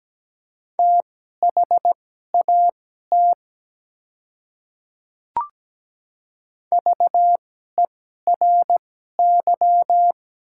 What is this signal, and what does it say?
Morse code, heard as a keyed tone, about 17 wpm, spaced as for 12 wpm, 700 Hz. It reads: THAT VERY
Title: 2x Character Spacing - Top 200 Words - 17wpm